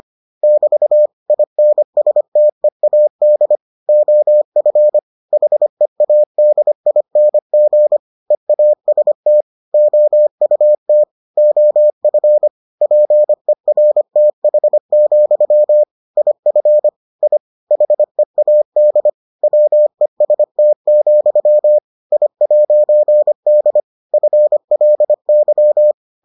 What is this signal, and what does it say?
Morse code, heard as a keyed tone, about 25 wpm, 600 Hz. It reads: <BT> INSTEAD OF HEADING EAST OUT OF PERTH, IF I HEAD WEST, I'D FLY